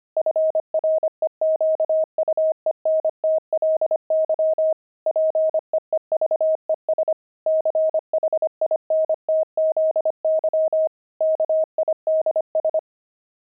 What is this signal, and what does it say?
Morse code, heard as a keyed tone, about 25 wpm, 630 Hz. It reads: FREQUENTLY PEEVES CHINTZY KIDS
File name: pangram-fs-25-cs-25-linoleum